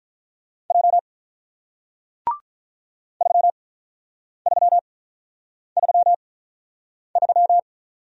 Morse code, heard as a keyed tone, about 48 wpm, spaced as for 9 wpm, 700 Hz. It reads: J 3333